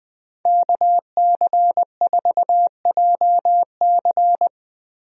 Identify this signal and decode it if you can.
Morse code, heard as a keyed tone, about 20 wpm, 700 Hz. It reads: KC4JC
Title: USA Call Signs - 5 Characters - 20wpm